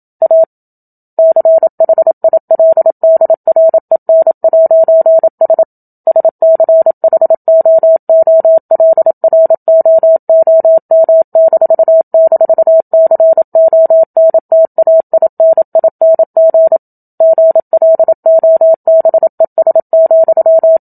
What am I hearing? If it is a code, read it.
Morse code, heard as a keyed tone, about 27 wpm, 650 Hz. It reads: A CHILDREN'S SCHOOLROOM--CONTAINING GLOBES,
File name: Book Two / 01-Chapter 1 - 27 WPM